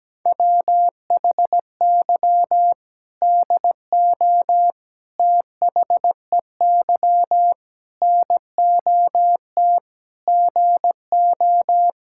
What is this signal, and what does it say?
Morse code, heard as a keyed tone, about 17 wpm, 700 Hz. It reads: WHY DO THEY NOT GO
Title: Sentences from Top 1600 Words 17wpm